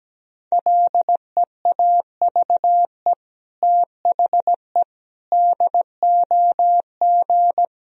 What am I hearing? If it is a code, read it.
Morse code, heard as a keyed tone, about 17 wpm, 700 Hz. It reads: LEAVE THE DOG
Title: Sentences from Top 1000 Words - Review Part 2 - 17wpm